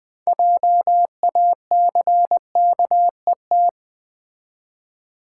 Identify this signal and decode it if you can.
Morse code, heard as a keyed tone, about 20 wpm, 700 Hz. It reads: JACKET
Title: 6 Letter Words - First 3 Letters Given - 20wpm